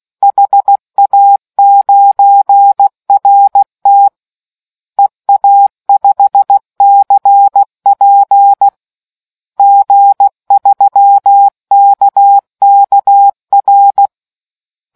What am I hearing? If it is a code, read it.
Morse code, heard as a keyed tone, about 16 wpm, 800 Hz. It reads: HA9RT EA5CP G3KKR